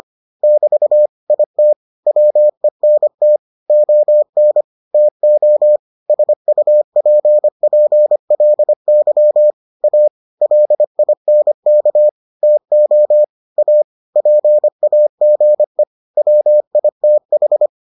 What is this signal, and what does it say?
Morse code, heard as a keyed tone, about 25 wpm, 600 Hz. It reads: <BT> IT WENT ON TO SUPPLY A LINK TO A PAGE WITH